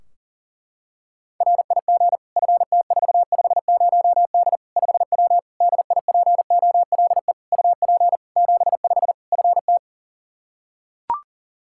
Morse code, heard as a keyed tone, about 40 wpm, 700 Hz. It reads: RIG FT450D 5W DIPOLE UP 75 FT
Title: Practice QSO - Exchange 2 - Info - 40wpm